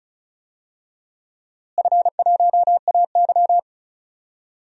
Morse code, heard as a keyed tone, about 36 wpm, 700 Hz. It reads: F1AY
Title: DXCC-100 Call Signs - 4 Characters - Repeated 3x - 35wpm